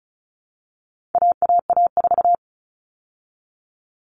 Morse code, heard as a keyed tone, about 34 wpm, 700 Hz. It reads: AAA4
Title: All Cut Numbers - 4-Digit 35wpm